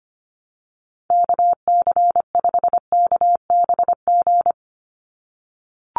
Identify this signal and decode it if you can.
Morse code, heard as a keyed tone, about 25 wpm, 700 Hz. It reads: KC5KBG
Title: USA Call Signs 25wpm